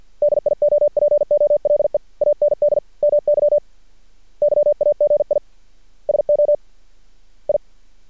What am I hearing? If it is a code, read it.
Morse code, heard as a keyed tone, about 48 wpm, 590 Hz. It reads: DIOPOLE AND MY YAGI SO I
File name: archived / CRSnet